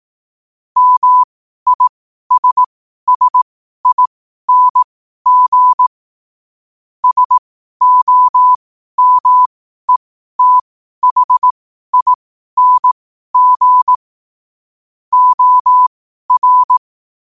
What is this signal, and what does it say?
Morse code, heard as a keyed tone, about 18 wpm, spaced as for 12 wpm, 1000 Hz. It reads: MISSING SOMETHING OR
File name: qotdfiles / 12wpm